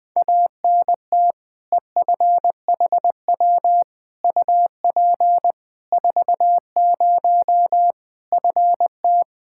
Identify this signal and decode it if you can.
Morse code, heard as a keyed tone, about 20 wpm, 700 Hz. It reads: ANT EFHW UP 40 FT
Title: K1USN SST QSO Practice 20wpm